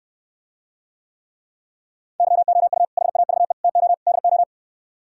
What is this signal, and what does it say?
Morse code, heard as a keyed tone, about 68 wpm, 700 Hz. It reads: YOU HAVE TO DO